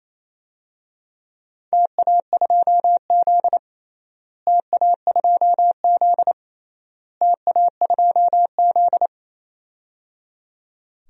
Morse code, heard as a keyed tone, about 27 wpm, 700 Hz. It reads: TA2Z TA2Z TA2Z
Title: DXCC-100 Call Signs - 4 Characters - Repeated 3x - 28wpm